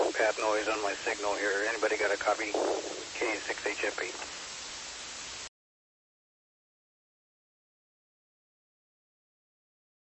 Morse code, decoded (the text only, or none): none